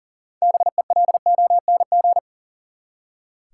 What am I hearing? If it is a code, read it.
Morse code, heard as a keyed tone, about 40 wpm, 700 Hz. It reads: BELONG
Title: Six Letter Words - 40wpm